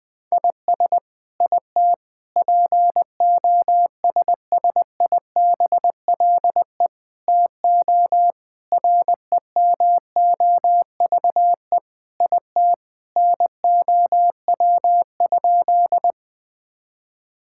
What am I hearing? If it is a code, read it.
Morse code, heard as a keyed tone, about 20 wpm, 700 Hz. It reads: IS IT POSSIBLE TO REMOVE IT NOW?
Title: Sentences from Top 1000 Words - Review Part 5 - 20wpm